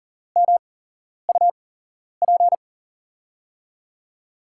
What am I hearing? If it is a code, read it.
Morse code, heard as a keyed tone, about 40 wpm, spaced as for 11 wpm, 700 Hz. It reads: MUP
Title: USA Call Signs - 8x Character Spacing - 40wpm